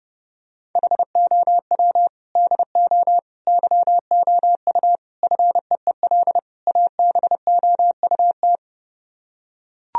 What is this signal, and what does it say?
Morse code, heard as a keyed tone, about 30 wpm, 700 Hz. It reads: HOW DO YOU FEEL ABOUT